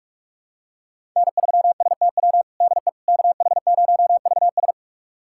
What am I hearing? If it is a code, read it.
Morse code, heard as a keyed tone, about 46 wpm, 700 Hz. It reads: N3STW DE KH0VS
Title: USA Callsign DE Callsign - 5 Characters - 45wpm